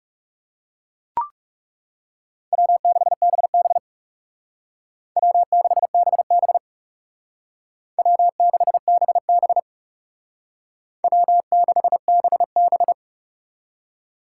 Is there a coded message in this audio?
Yes: Morse code from 2.5 to 12.9 s, heard as a keyed tone, about 38 wpm, 700 Hz. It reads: W6BB W6BB W6BB W6BB